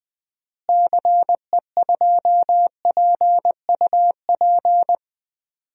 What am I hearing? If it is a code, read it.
Morse code, heard as a keyed tone, about 20 wpm, 700 Hz. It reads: CE2PUP
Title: DXCC-100 Call Signs - 6 Characters - Repeated 3x - 20wpm